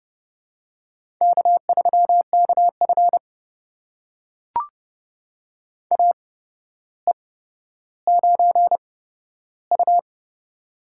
Morse code, heard as a keyed tone, about 30 wpm, spaced as for 13 wpm, 700 Hz. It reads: K3KF AE9U
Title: USA Call Signs - 4 Characters - 8x Character Spacing - 30wpm